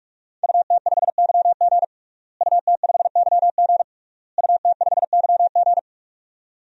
Morse code, heard as a keyed tone, about 41 wpm, 700 Hz. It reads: UT5YG UT5YG UT5YG
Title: DXCC-100 Call Signs - 5 Characters - Repeated 3x - 45wpm